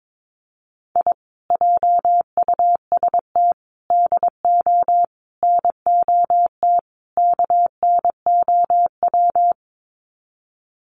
Morse code, heard as a keyed tone, about 22 wpm, 700 Hz. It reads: I JUST DO NOT KNOW
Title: Sets of 5 Words Encore 22wpm